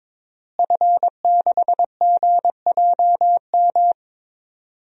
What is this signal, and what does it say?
Morse code, heard as a keyed tone, about 22 wpm, 700 Hz. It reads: F6GJM
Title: DXCC-100 Call Signs - 5 Characters - Repeated 3x - 22wpm